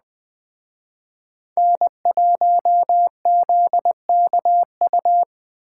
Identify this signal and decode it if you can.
Morse code, heard as a keyed tone, about 20 wpm, 700 Hz. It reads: N1ZKU
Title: Speed Racing - USA 1x3 Call Signs 17wpm